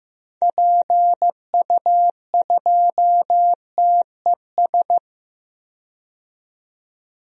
Morse code, heard as a keyed tone, about 15 wpm, 700 Hz. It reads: PU2TES
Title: DXCC-100 Call Signs - 6 Characters - 15wpm